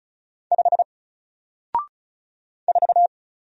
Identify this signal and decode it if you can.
Morse code, heard as a keyed tone, about 36 wpm, 700 Hz. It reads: 5 4
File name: Single-Letter-Number-35wpm